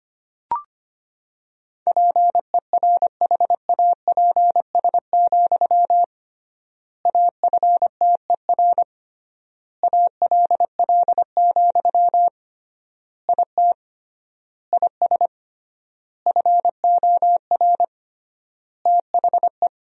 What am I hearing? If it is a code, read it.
Morse code, heard as a keyed tone, about 25 wpm, 700 Hz. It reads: PERHAPS, AFTER ALL, IT IS FOR THE
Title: 3x Word-Spacing - Sentences from Top 500 Words - 25wpm